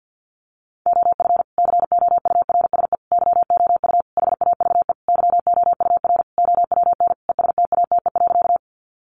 Morse code, heard as a keyed tone, about 48 wpm, 700 Hz. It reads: OF COURSE YOU HAVE YOUR OWN ESTATE.
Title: Sentences from Top 1200 Words 50wpm